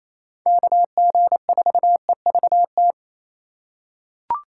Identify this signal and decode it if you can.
Morse code, heard as a keyed tone, about 28 wpm, 700 Hz. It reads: KG4EVT
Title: USA Call Signs - 6 Characters - 8x Character Spacing - 28wpm